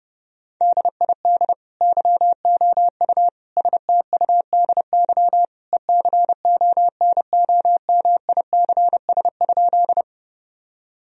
Morse code, heard as a keyed tone, about 30 wpm, 700 Hz. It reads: DID YOU STUDY ECONOMICS?
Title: Sentences from Top 1000 Words - Review Part 3 - 30wpm